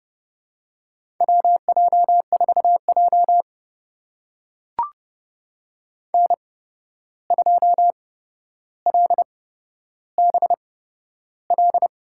Morse code, heard as a keyed tone, about 30 wpm, spaced as for 12 wpm, 700 Hz. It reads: WJ4J N2LBL
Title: USA Call Signs - 8x Character Spacing - 30wpm